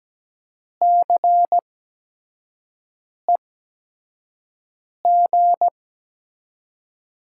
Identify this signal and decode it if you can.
Morse code, heard as a keyed tone, about 17 wpm, spaced as for 5 wpm, 700 Hz. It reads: CEG
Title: USA Call Signs - 6 Characters - 8x Character Spacing - 17wpm